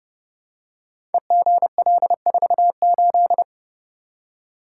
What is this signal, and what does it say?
Morse code, heard as a keyed tone, about 30 wpm, 700 Hz. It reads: EGL48
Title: Code Groups - 5 Letter:Numbers - 30wpm